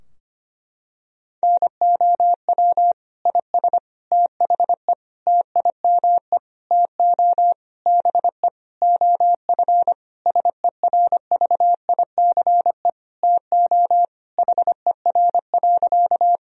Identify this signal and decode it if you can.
Morse code, heard as a keyed tone, about 25 wpm, 700 Hz. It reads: NOW IS THE TIME TO BE OF SERVICE TO HER.
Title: Sentences from Top 1000 Words - Review Part 4 - 25wpm